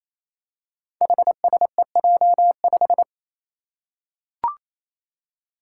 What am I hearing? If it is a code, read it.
Morse code, heard as a keyed tone, about 29 wpm, 700 Hz. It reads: HSEJ5
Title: Five Challenging Letter-Number Combinations 28wpm